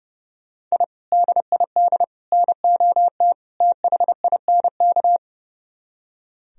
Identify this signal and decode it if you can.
Morse code, heard as a keyed tone, about 38 wpm, 700 Hz. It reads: I DID NOT THINK
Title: Speed-Racing - Sets of 4 Words 30wpm